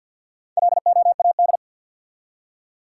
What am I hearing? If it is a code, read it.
Morse code, heard as a keyed tone, about 47 wpm, 700 Hz. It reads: LOAD
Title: Four Letter Words - 50wpm